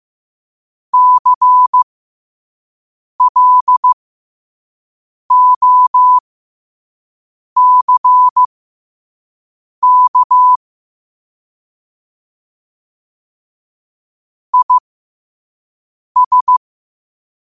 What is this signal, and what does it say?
Morse code, heard as a keyed tone, about 15 wpm, spaced as for 5 wpm, 1000 Hz. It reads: CLOCK IS
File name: qotdfiles / 5wpm